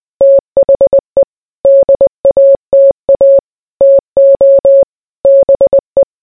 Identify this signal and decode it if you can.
Morse code, heard as a keyed tone, about 20 wpm, 550 Hz. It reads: THE DATA TO BE